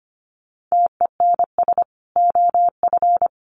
Morse code, heard as a keyed tone, about 25 wpm, 700 Hz. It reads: TENS OF
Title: Sets of 5 Words Encore2 25wpm